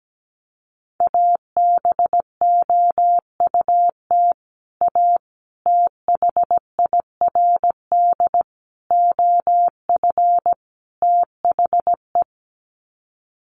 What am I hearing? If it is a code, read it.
Morse code, heard as a keyed tone, about 17 wpm, 700 Hz. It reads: ABOUT A THIRD OF THE
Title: Sets of 5 Words Encore2 17wpm